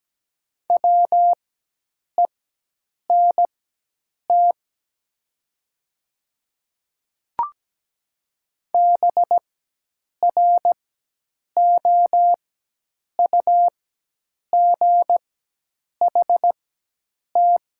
Morse code, heard as a keyed tone, about 17 wpm, spaced as for 8 wpm, 700 Hz. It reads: WENT BROUGHT
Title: 4x Character Spacing - Top 500 Words - 17wpm